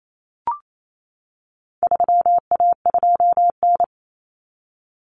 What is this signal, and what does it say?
Morse code, heard as a keyed tone, about 28 wpm, 700 Hz. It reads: 3A2N
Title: Common Cut Numbers - 4-Digit 28wpm